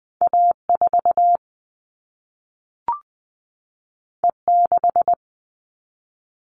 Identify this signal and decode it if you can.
Morse code, heard as a keyed tone, about 20 wpm, 700 Hz. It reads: A4 E6
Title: All Cut Numbers - 2-Digit 20wpm